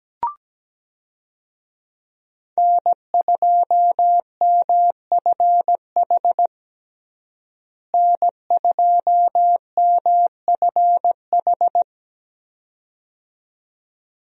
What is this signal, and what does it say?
Morse code, heard as a keyed tone, about 17 wpm, 700 Hz. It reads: N2MFH N2MFH
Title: USA Call Signs - 5 Characters - Repeated 2x - 17wpm